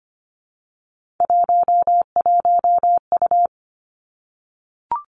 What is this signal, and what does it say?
Morse code, heard as a keyed tone, about 25 wpm, 700 Hz. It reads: 11U
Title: Three Challenging Letter-Number Combinations 25wpm